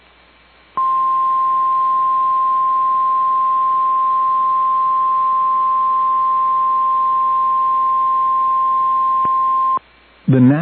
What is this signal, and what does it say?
Emergency Alert System (SAME) tones (the 1050 Hz attention tone), with speech. It is heard from 0.7 to 9.8 s.